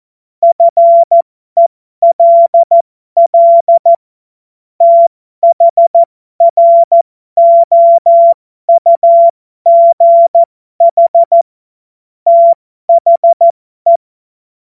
Morse code, heard as a keyed tone, about 14 wpm, spaced as for 12 wpm, 680 Hz. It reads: FELL THROUGH THE
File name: Morsekurs / Witze